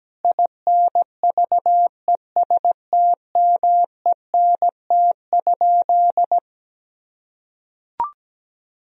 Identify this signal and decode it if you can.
Morse code, heard as a keyed tone, about 17 wpm, 700 Hz. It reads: INVESTMENT?
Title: Sentences from Top 1000 Words - Review Part 4 - 17wpm